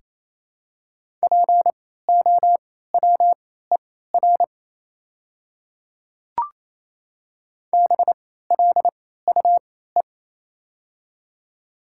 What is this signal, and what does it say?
Morse code, heard as a keyed tone, about 28 wpm, spaced as for 16 wpm, 700 Hz. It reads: POWER BLUE
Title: 3x Character Spacing - Top 500 Words - 28wpm